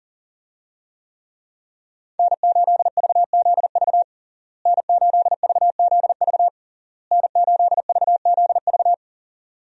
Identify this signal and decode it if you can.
Morse code, heard as a keyed tone, about 37 wpm, 700 Hz. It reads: N8VZV N8VZV N8VZV
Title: USA Call Signs - 5 Characters - Repeated 3x - 40wpm